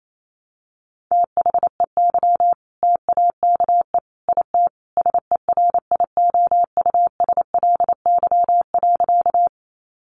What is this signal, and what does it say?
Morse code, heard as a keyed tone, about 28 wpm, 700 Hz. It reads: THEY TAKE IT SERIOUSLY.
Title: Sentences from Top 1200 Words 28wpm